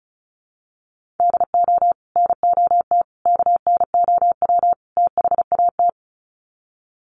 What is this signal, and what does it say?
Morse code, heard as a keyed tone, about 34 wpm, 700 Hz. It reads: DO NOT KNOW THAT
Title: Sets of 4 Words 35wpm